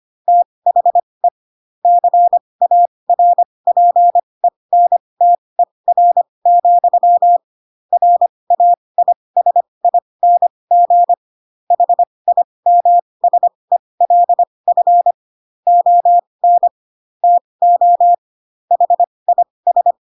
Morse code, heard as a keyed tone, about 25 wpm, spaced as for 20 wpm, 700 Hz. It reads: THE CARPENTER, RAISING HIMSELF ON TO HIS
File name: Pinocchio / Pinocchio ch2 15 25 ik1hge